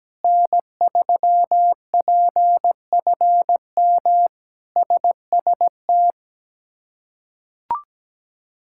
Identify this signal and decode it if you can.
Morse code, heard as a keyed tone, about 17 wpm, 700 Hz. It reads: N3PFM SST
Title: K1USN SST QSO Practice - 17wpm